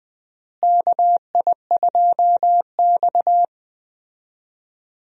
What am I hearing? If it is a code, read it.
Morse code, heard as a keyed tone, about 20 wpm, 700 Hz. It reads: KI2X
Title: USA Call Signs - 4 Characters - Repeated 3x - 20wpm